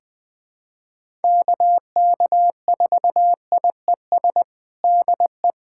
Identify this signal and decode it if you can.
Morse code, heard as a keyed tone, about 20 wpm, 700 Hz. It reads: KK4IES DE
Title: USA Callsign DE Callsign 20wpm